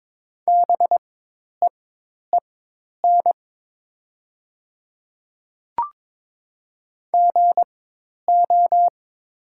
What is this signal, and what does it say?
Morse code, heard as a keyed tone, about 22 wpm, spaced as for 10 wpm, 700 Hz. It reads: BEEN GO
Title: 4x Character Spacing - Top 200 Words - 22wpm